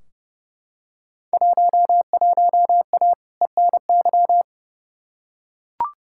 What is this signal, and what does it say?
Morse code, heard as a keyed tone, about 30 wpm, 700 Hz. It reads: 11A ENY
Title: Field Day Exchanges Encore - 30wpm